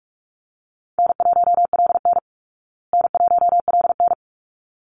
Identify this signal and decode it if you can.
Morse code, heard as a keyed tone, about 44 wpm, 700 Hz. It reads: N1LN N1LN
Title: Easy Call Signs Repeated 2x - 45wpm